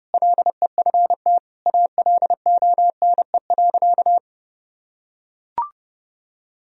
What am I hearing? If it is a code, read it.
Morse code, heard as a keyed tone, about 30 wpm, 700 Hz. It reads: LEFT ALONE.
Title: Sentences from Top 1000 Words - Review Part 1 - 30wpm